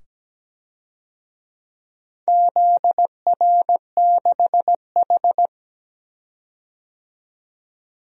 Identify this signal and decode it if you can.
Morse code, heard as a keyed tone, about 17 wpm, 700 Hz. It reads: ZR6H
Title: DXCC-100 Call Signs - 4 Characters - 17wpm